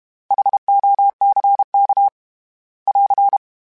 Morse code, heard as a keyed tone, about 32 wpm, 800 Hz. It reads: HOCK <AR>
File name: H Prefixes / H mp3s